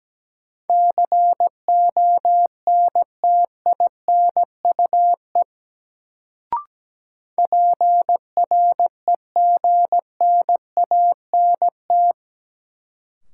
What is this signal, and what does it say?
Morse code, heard as a keyed tone, about 17 wpm, 700 Hz. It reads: CONTINUE PREGNANT
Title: Sentences from Top 1000 Words - Review Part 6 - New Word Drill - 17wpm